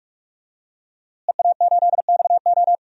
Morse code, heard as a keyed tone, about 41 wpm, 700 Hz. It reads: EA8XO
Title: DXCC-100 Call Signs - 5 Characters - Repeated 3x - 45wpm